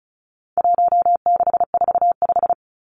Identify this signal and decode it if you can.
Morse code, heard as a keyed tone, about 34 wpm, 700 Hz. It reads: 1645
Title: Numbers - 4-Digit 35wpm